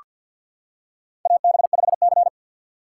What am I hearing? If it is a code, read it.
Morse code, heard as a keyed tone, about 47 wpm, 700 Hz. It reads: AB5C
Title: K1USN SST QSO Practice - 50wpm